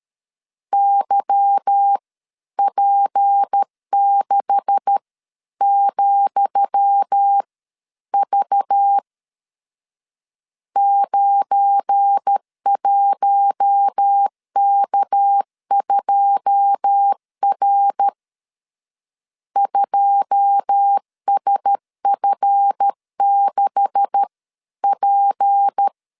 Morse code, heard as a keyed tone, about 13 wpm, spaced as for 11 wpm, 800 Hz. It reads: YP6,V 91K2R 2SF6P
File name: LEZIONI / 10 SIGNS / 10D COMA 12wpm 2544